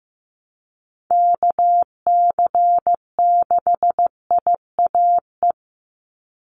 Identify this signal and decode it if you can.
Morse code, heard as a keyed tone, about 15 wpm, 700 Hz. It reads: KC6IAE
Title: USA Call Signs 15wpm